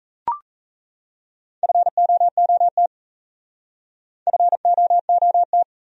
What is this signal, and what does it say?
Morse code, heard as a keyed tone, about 36 wpm, 700 Hz. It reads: FOOT FOOT